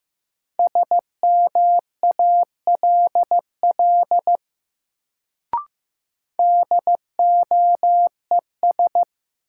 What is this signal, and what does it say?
Morse code, heard as a keyed tone, about 15 wpm, 700 Hz. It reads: SMALL DOES
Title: Sentences from Top 200 Words - New Word Drill - 15wpm